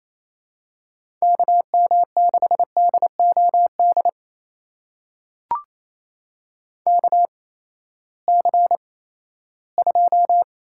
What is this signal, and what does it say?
Morse code, heard as a keyed tone, about 28 wpm, spaced as for 16 wpm, 700 Hz. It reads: KM6DOD KC2
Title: USA Call Signs - 6 Characters - 8x Character Spacing - 28wpm